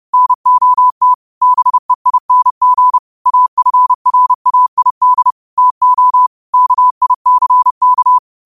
Morse code, heard as a keyed tone, about 30 wpm, 1000 Hz. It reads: NOT BEING AFRAID TO KICK
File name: qotdfiles / 30wpm